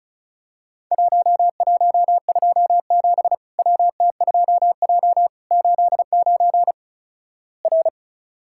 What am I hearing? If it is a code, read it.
Morse code, heard as a keyed tone, about 34 wpm, 700 Hz. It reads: 1127 WT2J 89 R
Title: Worked All Europe DX Contest - QTC Exchange - 35wpm